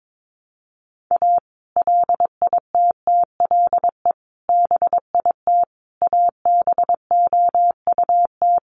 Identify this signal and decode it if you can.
Morse code, heard as a keyed tone, about 22 wpm, 700 Hz. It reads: A LITTLE BIT ABOUT
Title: Sets of 5 Words Encore2 22wpm